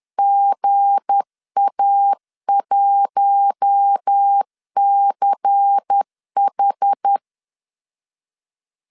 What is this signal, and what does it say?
Morse code, heard as a keyed tone, about 10 wpm, 800 Hz. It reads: GA1CH
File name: CW / DET / 1 9 1500C S A 10wpm 4511